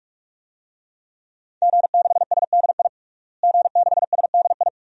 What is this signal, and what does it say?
Morse code, heard as a keyed tone, about 46 wpm, 700 Hz. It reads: G6SDI G6SDI
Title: DXCC-100 Call Signs - 5 Characters - Repeated 2x - 45wpm